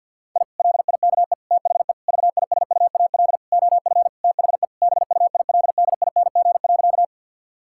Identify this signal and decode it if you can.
Morse code, heard as a keyed tone, about 50 wpm, 700 Hz. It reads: I LIKE THE VISUAL OF THE BUILDING.
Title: Sentences from Top 1000 Words - Review Part 6 - 50wpm